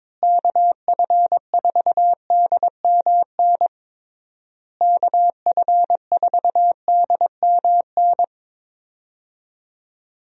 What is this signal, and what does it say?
Morse code, heard as a keyed tone, about 22 wpm, 700 Hz. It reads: KF4DMN KF4DMN
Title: USA Call Signs - 6 Characters - Repeated 3x - 22wpm